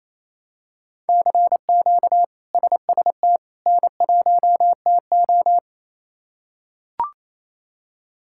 Morse code, heard as a keyed tone, about 28 wpm, 700 Hz. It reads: CQ SST N1TO
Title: K1USN SST QSO Practice - 28wpm